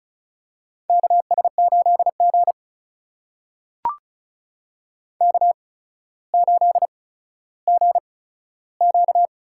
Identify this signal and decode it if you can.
Morse code, heard as a keyed tone, about 36 wpm, spaced as for 15 wpm, 700 Hz. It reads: KS8G K8GQ
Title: USA Call Signs - 4 Characters - 8x Character Spacing - 35wpm